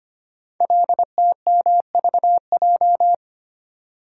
Morse code, heard as a keyed tone, about 25 wpm, 700 Hz. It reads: LTMVJ
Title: Code Groups - 5 Letters - 25wpm